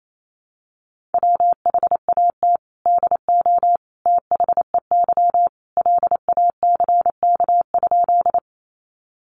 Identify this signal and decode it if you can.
Morse code, heard as a keyed tone, about 28 wpm, 700 Hz. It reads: WHAT DO THEY LACK?
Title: Sentences from Top 1100 Words 28wpm